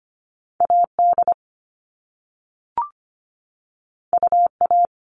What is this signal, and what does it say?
Morse code, heard as a keyed tone, about 25 wpm, 700 Hz. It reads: AD UA
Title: All Cut Numbers - 2-Digit 25wpm